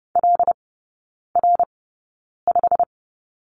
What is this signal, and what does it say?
Morse code, heard as a keyed tone, about 30 wpm, spaced as for 9 wpm, 700 Hz. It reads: LR5
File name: course / 30:10wpm / 087 Lesson 15e - Number 5 - Three Characters 30wpm